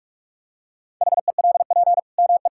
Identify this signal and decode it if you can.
Morse code, heard as a keyed tone, about 41 wpm, 700 Hz. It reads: HELP ME
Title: Sentences from Top 1000 Words - Review Part 5 - 45wpm